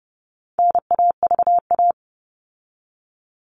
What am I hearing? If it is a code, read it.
Morse code, heard as a keyed tone, about 30 wpm, 700 Hz. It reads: NAVA
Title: All Cut Numbers - 4-Digit 30wpm